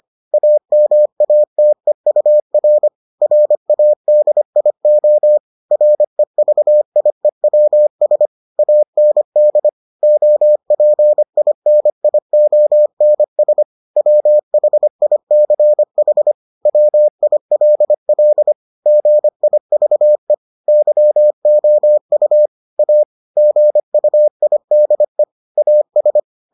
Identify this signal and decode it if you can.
Morse code, heard as a keyed tone, about 25 wpm, 600 Hz. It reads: AMATEUR RADIO REVIEWS AND OPINIONS WHICH WILL GIVE YOU A GUIDE AS